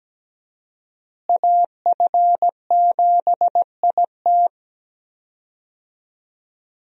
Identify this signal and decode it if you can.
Morse code, heard as a keyed tone, about 17 wpm, 700 Hz. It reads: AF7IT
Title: USA Call Signs - 5 Characters - Repeated 3x - 17wpm